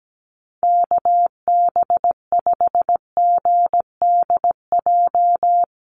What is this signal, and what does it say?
Morse code, heard as a keyed tone, about 17 wpm, 700 Hz. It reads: KB5GDJ
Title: USA Call Signs Repeated 2x 17wpm